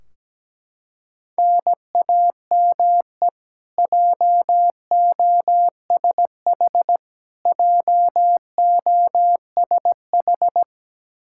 Morse code, heard as a keyed tone, about 17 wpm, 700 Hz. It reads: NAME JOSH JOSH
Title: K1USN SST QSO Practice 17wpm